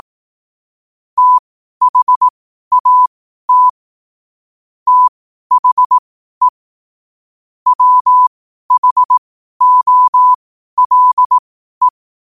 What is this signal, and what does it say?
Morse code, heard as a keyed tone, about 18 wpm, spaced as for 12 wpm, 1000 Hz. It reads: THAT THE WHOLE
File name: qotdfiles / 12wpm